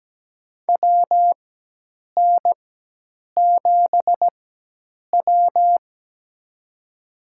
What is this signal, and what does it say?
Morse code, heard as a keyed tone, about 17 wpm, spaced as for 8 wpm, 700 Hz. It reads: WN7W
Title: USA Call Signs - 4 Characters - 4x Character Spacing - 17wpm